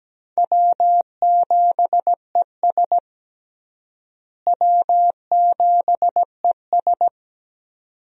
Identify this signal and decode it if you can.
Morse code, heard as a keyed tone, about 17 wpm, 700 Hz. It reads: W7ES W7ES
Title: USA Call Signs - 4 Characters - Repeated 2x - 17wpm